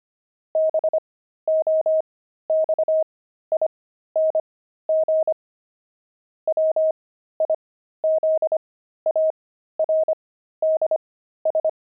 Morse code, heard as a keyed tone, about 25 wpm, spaced as for 13 wpm, 630 Hz. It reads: BOXING WIZARDS
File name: pangram-fs-13-cs-25-jumping